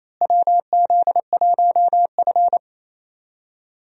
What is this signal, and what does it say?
Morse code, heard as a keyed tone, about 28 wpm, 700 Hz. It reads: WZ1F
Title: USA Call Signs - 4 Characters - Repeated 3x - 28wpm